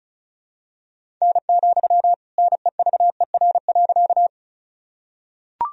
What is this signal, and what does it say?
Morse code, heard as a keyed tone, about 36 wpm, 700 Hz. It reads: N, NEVER.
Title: Sentences from Top 300 Words and Abbreviations 35wpm